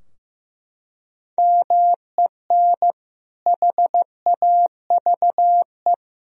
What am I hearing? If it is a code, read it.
Morse code, heard as a keyed tone, about 16 wpm, 700 Hz. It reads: MEN HAVE
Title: Sentences from Top 1000 Words - Review Part 3 - 15wpm